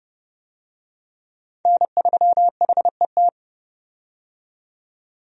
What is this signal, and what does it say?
Morse code, heard as a keyed tone, about 30 wpm, 700 Hz. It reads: N3HET
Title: USA Call Signs - 5 Characters - 30wpm